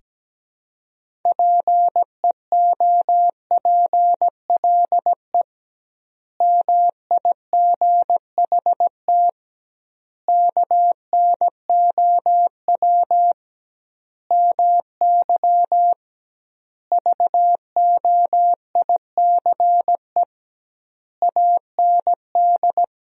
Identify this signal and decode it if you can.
Morse code, heard as a keyed tone, about 17 wpm, 700 Hz. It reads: PEOPLE MIGHT KNOW MY VOICE AND
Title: 2x Word-Spacing - Sentences from Top 500 Words - 17wpm